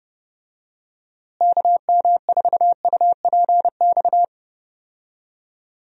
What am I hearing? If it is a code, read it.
Morse code, heard as a keyed tone, about 29 wpm, 700 Hz. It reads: KM4UPX
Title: USA Call Signs - 6 Characters - Repeated 2x - 30wpm